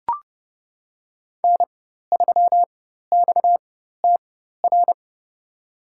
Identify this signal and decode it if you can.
Morse code, heard as a keyed tone, about 30 wpm, spaced as for 14 wpm, 700 Hz. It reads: N3XTR
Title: USA Call Signs - 5 Characters - 4x Character Spacing - 30wpm